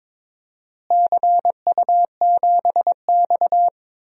Morse code, heard as a keyed tone, about 22 wpm, 700 Hz. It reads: CU7X
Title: DXCC-100 Call Signs - 4 Characters - Repeated 2x - 22wpm